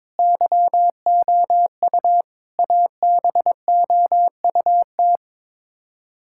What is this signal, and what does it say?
Morse code, heard as a keyed tone, about 22 wpm, 700 Hz. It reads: YOU ABOUT
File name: practice / 22wpm / Build A Phrase - 5-Word Phrases - 22wpm